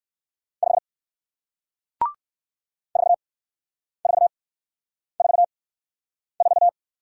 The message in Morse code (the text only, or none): 5 4444